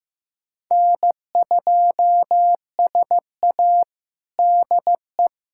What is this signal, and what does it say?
Morse code, heard as a keyed tone, about 15 wpm, 700 Hz. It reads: N2SA DE
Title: USA Callsign DE Callsign 15wpm